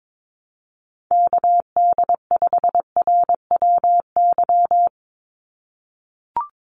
Morse code, heard as a keyed tone, about 22 wpm, 700 Hz. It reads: KD5RWY
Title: USA Call Signs 22wpm